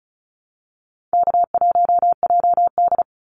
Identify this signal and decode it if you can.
Morse code, heard as a keyed tone, about 34 wpm, 700 Hz. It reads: K1JD
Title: NA SOTA Activator Callsigns 35wpm